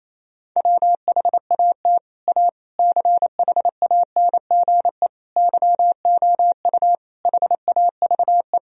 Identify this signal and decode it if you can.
Morse code, heard as a keyed tone, about 27 wpm, 700 Hz. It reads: WHAT A CHANGE YOU HAVE
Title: Sentences from Top 1000 Words - Review Part 1 - 28wpm